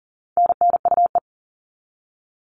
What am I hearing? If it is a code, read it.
Morse code, heard as a keyed tone, about 40 wpm, 700 Hz. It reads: NNUE
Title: All Cut Numbers - 4-Digit 40wpm